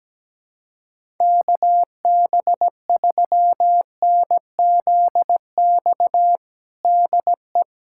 Morse code, heard as a keyed tone, about 17 wpm, 700 Hz. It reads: KB3NZX DE
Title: USA Callsign DE Callsign 17wpm